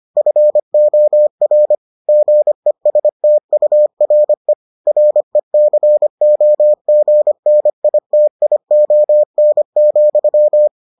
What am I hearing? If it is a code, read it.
Morse code, heard as a keyed tone, about 25 wpm, 600 Hz. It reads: FOR GESTURE RECOGNITION,